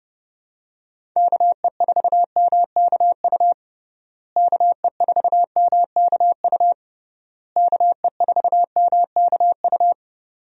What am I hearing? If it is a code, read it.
Morse code, heard as a keyed tone, about 30 wpm, 700 Hz. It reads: KE4MKU KE4MKU KE4MKU
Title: USA Call Signs - 6 Characters - Repeated 3x - 30wpm